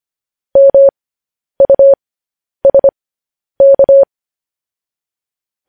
Morse code, heard as a keyed tone, about 25 wpm, spaced as for 10 wpm, 550 Hz. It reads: MUSK
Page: news headlines in morse code at 10 wpm